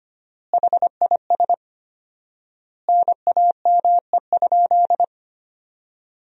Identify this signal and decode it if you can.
Morse code, heard as a keyed tone, about 25 wpm, 700 Hz. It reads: HIS NAME?
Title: 4x Word-Spacing - Sentences from Top 200 Words - 25wpm